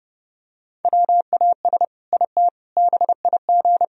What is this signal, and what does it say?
Morse code, heard as a keyed tone, about 30 wpm, 700 Hz. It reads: WAS IT BIG
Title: Sentences from Top 1000 Words - Review Part 1 - 30wpm